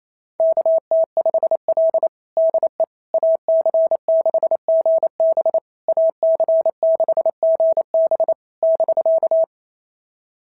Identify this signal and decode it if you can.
Morse code, heard as a keyed tone, about 25 wpm, 650 Hz. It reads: KT5L DE AC6GB AC6GB <BK>
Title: SOTA QSO Practice 28wpm